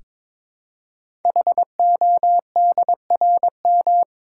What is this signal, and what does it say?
Morse code, heard as a keyed tone, about 23 wpm, 700 Hz. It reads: HODRM
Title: Code Groups - 5 Letters - 22wpm